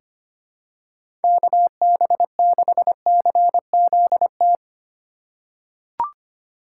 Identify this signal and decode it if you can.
Morse code, heard as a keyed tone, about 25 wpm, 700 Hz. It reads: KB6CZT
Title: USA Call Signs - 6 Characters - 3x Character Spacing - 25wpm